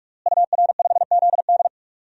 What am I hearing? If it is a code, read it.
Morse code, heard as a keyed tone, about 47 wpm, 700 Hz. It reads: UR5ZD
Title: DXCC-100 Call Signs - 5 Characters - Repeated 2x - 45wpm